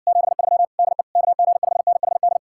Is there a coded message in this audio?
Yes: Morse code from 0.1 to 2.4 s, heard as a keyed tone, about 59 wpm, 700 Hz. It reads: 73 DE KG5NHD